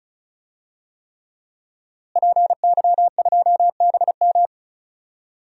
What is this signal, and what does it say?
Morse code, heard as a keyed tone, about 34 wpm, 700 Hz. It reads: PY2BM
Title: DXCC-100 Call Signs - 5 Characters - Repeated 3x - 35wpm